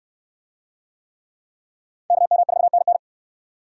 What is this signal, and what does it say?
Morse code, heard as a keyed tone, about 63 wpm, 700 Hz. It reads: KM4NN